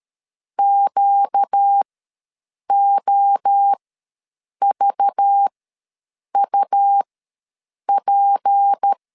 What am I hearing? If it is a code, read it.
Morse code, heard as a keyed tone, about 13 wpm, spaced as for 7 wpm, 800 Hz. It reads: QOVUP
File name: LEZIONI / 7 GF9Z / 7 GF ws 12wpm 3140